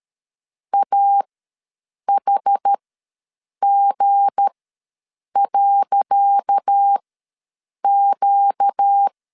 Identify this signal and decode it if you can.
Morse code, heard as a keyed tone, about 13 wpm, spaced as for 7 wpm, 800 Hz. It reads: AHG.Q
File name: LEZIONI / 10 SIGNS / 10C DOT ws 12wpm 3548